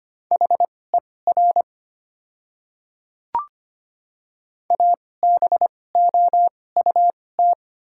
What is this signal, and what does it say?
Morse code, heard as a keyed tone, about 25 wpm, spaced as for 18 wpm, 700 Hz. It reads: HER ABOUT
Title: 2x Character Spacing - Top 200 Words - 25wpm